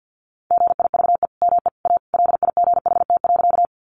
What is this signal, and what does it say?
Morse code, heard as a keyed tone, about 50 wpm, 700 Hz. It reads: GIVE ME A LIGHT.
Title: Sentences from Top 200 Words 50wpm